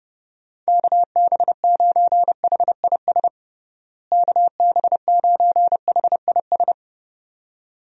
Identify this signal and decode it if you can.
Morse code, heard as a keyed tone, about 32 wpm, 700 Hz. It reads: KB9HIS KB9HIS
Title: USA Call Signs - 6 Characters - Repeated 2x - 30wpm